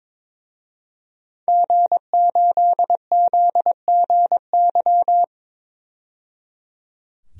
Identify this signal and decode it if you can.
Morse code, heard as a keyed tone, about 22 wpm, 700 Hz. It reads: G8ZGY
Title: DXCC-100 Call Signs - 5 Characters - 22wpm